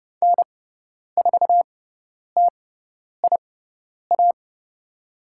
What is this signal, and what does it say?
Morse code, heard as a keyed tone, about 30 wpm, spaced as for 10 wpm, 700 Hz. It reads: N4TIA
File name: Lesson 18g - Letter m - Call Signs 30wpm